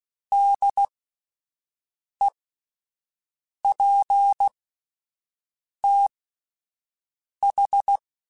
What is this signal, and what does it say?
Morse code, heard as a keyed tone, about 16 wpm, spaced as for 5 wpm, 800 Hz. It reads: DEPTH